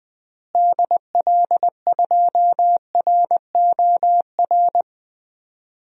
Morse code, heard as a keyed tone, about 20 wpm, 700 Hz. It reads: DL2ROR
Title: DXCC-100 Call Signs - 6 Characters - Repeated 2x - 20wpm